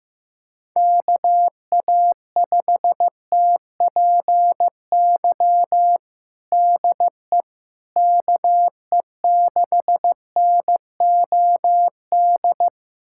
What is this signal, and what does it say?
Morse code, heard as a keyed tone, about 15 wpm, 700 Hz. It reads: KA5TPY DE KE6NOD
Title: USA Callsign DE Callsign - 6 Characters - 15wpm